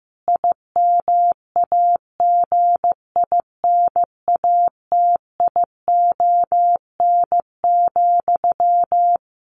Morse code, heard as a keyed tone, about 15 wpm, 700 Hz. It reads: IMAGINATION,
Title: Sentences from Top 1300 Words 15wpm